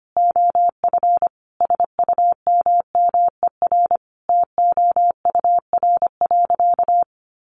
Morse code, heard as a keyed tone, about 25 wpm, 700 Hz. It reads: OF SUMMER TOUR.
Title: Sentences from Top 1400 Words 25wpm